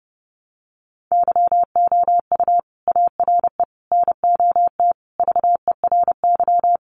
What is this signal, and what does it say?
Morse code, heard as a keyed tone, about 30 wpm, 700 Hz. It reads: YOU ARE NOT VERY
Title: Sentences from Top 1200 Words 30wpm